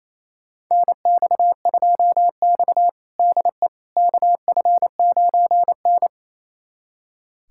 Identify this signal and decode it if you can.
Morse code, heard as a keyed tone, about 28 wpm, 700 Hz. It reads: NX2X DE KF9N
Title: USA Callsign DE Callsign - 4 Characters - 28wpm